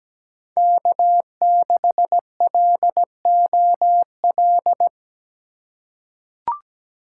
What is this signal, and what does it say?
Morse code, heard as a keyed tone, about 17 wpm, 700 Hz. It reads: K6LOL
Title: USA Call Signs - 5 Characters - 2x Character Spacing - 17wpm